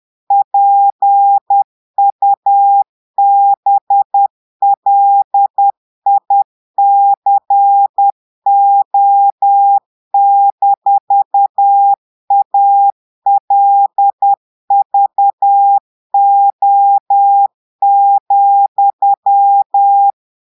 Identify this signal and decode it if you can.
Morse code, heard as a keyed tone, about 10 wpm, 800 Hz. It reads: PUBLICO-ALVO,
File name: portugais texte en clair 2 vitesse 10 mots minutes